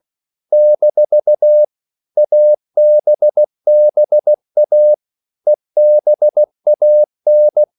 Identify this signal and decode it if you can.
Morse code, heard as a keyed tone, about 16 wpm, 600 Hz. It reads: - ABBA EBAN
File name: CW / Practice / EnglishQuotations0116